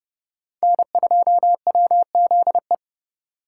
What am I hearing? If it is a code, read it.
Morse code, heard as a keyed tone, about 37 wpm, 700 Hz. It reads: N2WZE
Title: Speed Racing - USA 1x3 Call Signs 30wpm